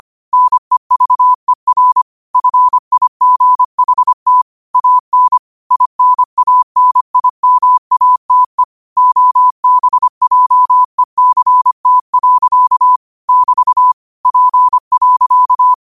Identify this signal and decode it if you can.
Morse code, heard as a keyed tone, about 25 wpm, 1000 Hz. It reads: NEVER FIGHT AN INANIMATE OBJECT. <BT> P.
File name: qotdfiles / 25wpm